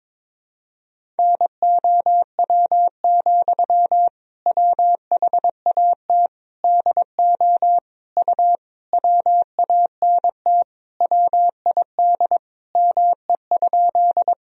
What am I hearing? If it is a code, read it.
Morse code, heard as a keyed tone, about 22 wpm, 700 Hz. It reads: NOW, WHAT DO U WANT WID ME?
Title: Sentences from Top 200 Words and Abbreviations 22wpm